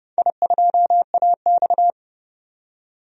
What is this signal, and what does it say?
Morse code, heard as a keyed tone, about 30 wpm, 700 Hz. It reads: I2AX
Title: DXCC-100 Call Signs - 4 Characters - 30wpm